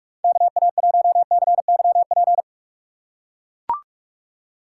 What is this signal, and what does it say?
Morse code, heard as a keyed tone, about 42 wpm, 700 Hz. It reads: KA1CYP
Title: USA Call Signs - 6 Characters - 4x Character Spacing - 45wpm